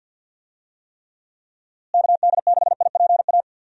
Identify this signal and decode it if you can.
Morse code, heard as a keyed tone, about 50 wpm, 700 Hz. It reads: KD6IPA
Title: USA Call Signs - 6 Characters - Repeated 2x - 50wpm